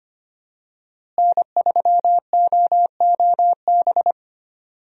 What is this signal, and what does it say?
Morse code, heard as a keyed tone, about 25 wpm, 700 Hz. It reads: N3OOB
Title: USA Call Signs - 5 Characters - Repeated 3x - 25wpm